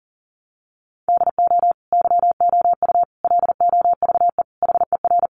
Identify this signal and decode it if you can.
Morse code, heard as a keyed tone, about 40 wpm, 700 Hz. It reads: DO YOU LOVE HER
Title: Sentences from Top 400 Words 40wpm